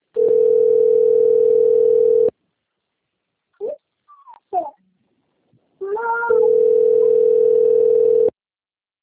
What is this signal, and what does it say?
Ringback tone (North America), 440 + 480 Hz.